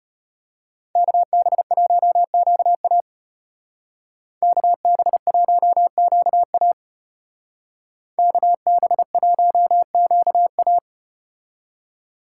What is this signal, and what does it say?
Morse code, heard as a keyed tone, about 34 wpm, 700 Hz. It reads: KB1QA KB1QA KB1QA